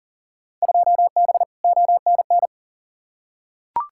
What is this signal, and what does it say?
Morse code, heard as a keyed tone, about 39 wpm, 700 Hz. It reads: 2B ONN
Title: Field Day Exchanges Encore - 40wpm